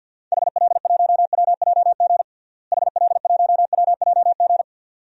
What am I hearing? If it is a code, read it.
Morse code, heard as a keyed tone, about 50 wpm, 700 Hz. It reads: HL1WJG HL1WJG
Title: DXCC-100 Call Signs - 6 Characters - Repeated 2x - 50wpm